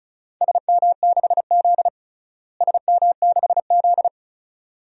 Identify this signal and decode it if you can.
Morse code, heard as a keyed tone, about 34 wpm, 700 Hz. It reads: SM6Z SM6Z
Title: DXCC-100 Call Signs - 4 Characters - Repeated 2x - 35wpm